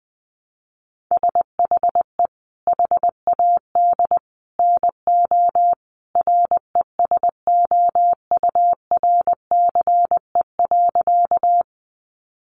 Morse code, heard as a keyed tone, about 20 wpm, 700 Hz. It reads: SHE HAD NO RESOURCE.
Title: Sentences from Top 1300 Words 20wpm